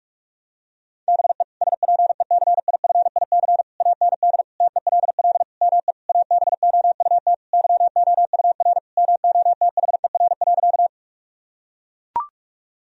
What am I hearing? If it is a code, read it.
Morse code, heard as a keyed tone, about 41 wpm, 700 Hz. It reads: BE SPECIFIC AND TELL ME ABOUT YOUR MOTHER.
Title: Sentences from Top 1000 Words - Review Part 3 - 45wpm